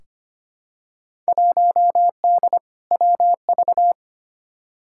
Morse code, heard as a keyed tone, about 25 wpm, 700 Hz. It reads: 1D WV
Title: Field Day Exchanges - 25wpm